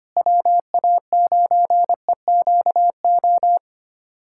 Morse code, heard as a keyed tone, about 25 wpm, 700 Hz. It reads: WA9EQO
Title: USA Callsign DE Callsign 25wpm